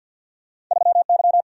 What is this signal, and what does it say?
Morse code, heard as a keyed tone, about 46 wpm, 700 Hz. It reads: 3Y
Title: Single-Letter Single-Number 50wpm